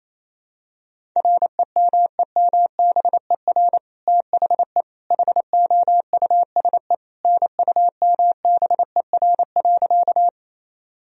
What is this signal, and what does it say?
Morse code, heard as a keyed tone, about 28 wpm, 700 Hz. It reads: REMEMBER THE HOUSE NUMBER.
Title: Sentences from Top 1000 Words - Review Part 1 - 28wpm